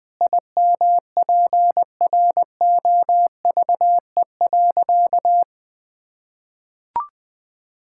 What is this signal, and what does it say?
Morse code, heard as a keyed tone, about 20 wpm, 700 Hz. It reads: IMPROVE.
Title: Sentences with 4-Word Phrases - 20wpm